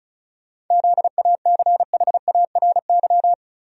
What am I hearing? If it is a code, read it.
Morse code, heard as a keyed tone, about 34 wpm, 700 Hz. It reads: ZACHARY
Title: USA Common Names - 35wpm